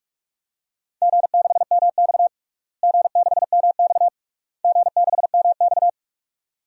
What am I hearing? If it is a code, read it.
Morse code, heard as a keyed tone, about 46 wpm, 700 Hz. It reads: G6MX G6MX G6MX
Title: DXCC-100 Call Signs - 4 Characters - Repeated 3x - 45wpm